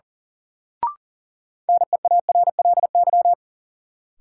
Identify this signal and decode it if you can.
Morse code, heard as a keyed tone, about 37 wpm, 700 Hz. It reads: NEARLY
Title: Sentences from Top 1300 Words - New Word Drill Alt - 40wpm